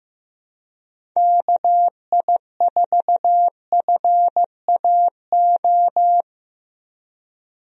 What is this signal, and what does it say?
Morse code, heard as a keyed tone, about 15 wpm, 700 Hz. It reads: KI4FAO
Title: USA Call Signs - 6 Characters - Repeated 3x - 15wpm